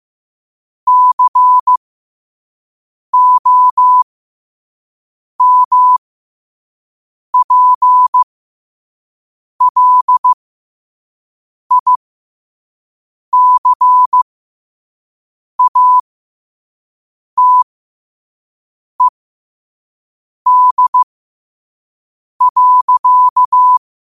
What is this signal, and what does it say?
Morse code, heard as a keyed tone, about 15 wpm, spaced as for 5 wpm, 1000 Hz. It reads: COMPLICATED.
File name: qotdfiles / 5wpm